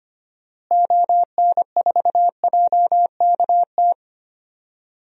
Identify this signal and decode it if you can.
Morse code, heard as a keyed tone, about 25 wpm, 700 Hz. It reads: ON4JKT
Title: DXCC-100 Call Signs - 6 Characters - Repeated 2x - 25wpm